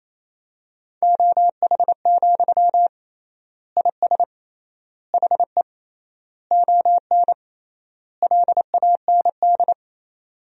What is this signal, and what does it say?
Morse code, heard as a keyed tone, about 28 wpm, 700 Hz. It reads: OH, IS HE ON LAND
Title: 3x Word-Spacing - Sentences from Top 500 Words - 28wpm